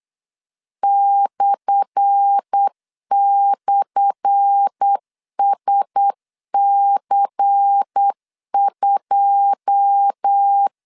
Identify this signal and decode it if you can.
Morse code, heard as a keyed tone, about 8 wpm, 800 Hz. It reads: //SC2
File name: LEZIONI / 10 SIGNS / 10A SLASH 8wpm 3452